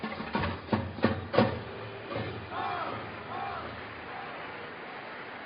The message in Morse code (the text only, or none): none